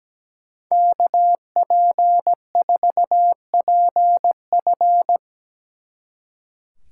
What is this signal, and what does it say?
Morse code, heard as a keyed tone, about 19 wpm, 700 Hz. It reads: KP4PF
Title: Speed Racing - USA 2x2 Call Signs 17wpm